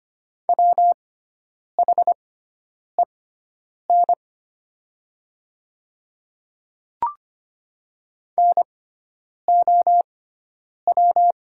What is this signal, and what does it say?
Morse code, heard as a keyed tone, about 25 wpm, spaced as for 9 wpm, 700 Hz. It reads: WHEN NOW
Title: 6x Character Spacing - Top 100 Words - 25wpm